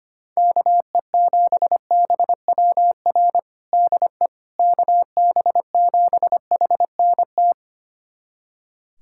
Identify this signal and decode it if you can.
Morse code, heard as a keyed tone, about 25 wpm, 700 Hz. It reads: KE7BWR DE KB7HNT
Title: USA Callsign DE Callsign - 6 Characters - 25wpm